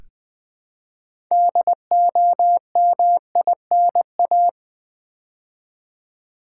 Morse code, heard as a keyed tone, about 20 wpm, 700 Hz. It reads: DOMINA
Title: Build A Word - 3 Letters At A Time - 8-Letter Words - 20wpm